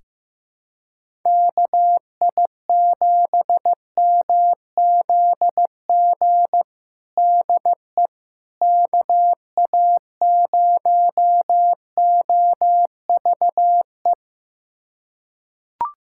Morse code, heard as a keyed tone, about 15 wpm, 700 Hz. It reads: KI7MZG DE KA0OVE
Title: USA Callsign DE Callsign 15wpm